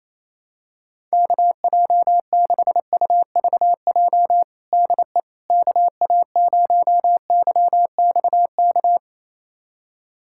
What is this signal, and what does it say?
Morse code, heard as a keyed tone, about 28 wpm, 700 Hz. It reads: KJ6UVJ DE KA0YXK
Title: USA Callsign DE Callsign - 6 Characters - 28wpm